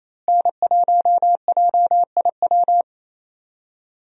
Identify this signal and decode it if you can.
Morse code, heard as a keyed tone, about 28 wpm, 700 Hz. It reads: N1JIW
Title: USA Call Signs - 5 Characters - Repeated 3x - 28wpm